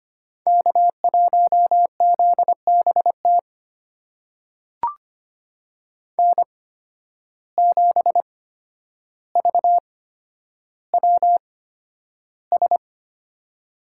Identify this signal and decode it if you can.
Morse code, heard as a keyed tone, about 25 wpm, spaced as for 11 wpm, 700 Hz. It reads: K1ZBT N7VWS
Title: USA Call Signs - 5 Characters - 8x Character Spacing - 25wpm